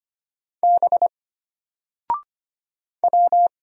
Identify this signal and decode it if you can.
Morse code, heard as a keyed tone, about 25 wpm, 700 Hz. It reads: B W
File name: Single-Letter-Number-25wpm